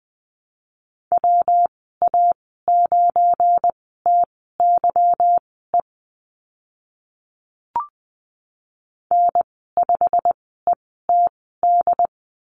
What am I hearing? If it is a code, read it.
Morse code, heard as a keyed tone, about 20 wpm, spaced as for 14 wpm, 700 Hz. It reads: WA9TYE N5ETD
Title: 186 Lesson 30g - Letter b - Call Signs 20wpm